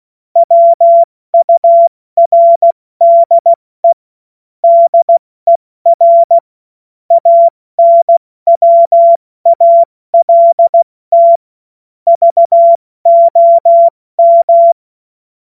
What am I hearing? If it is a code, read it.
Morse code, heard as a keyed tone, about 16 wpm, 680 Hz. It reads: WURDE DER ANWALT VOM
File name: Morsekurs / Witze / Witz 30. August 2022 70 F80